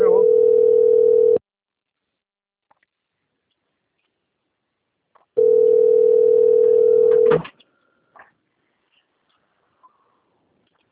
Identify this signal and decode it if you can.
Ringback tone (North America), 440 + 480 Hz.
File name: rg-1600-unknown